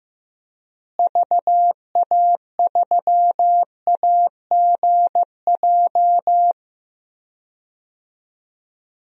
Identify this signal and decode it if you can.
Morse code, heard as a keyed tone, about 15 wpm, 700 Hz. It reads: VA3AGJ
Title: DXCC-100 Call Signs - 6 Characters - Repeated 3x - 15wpm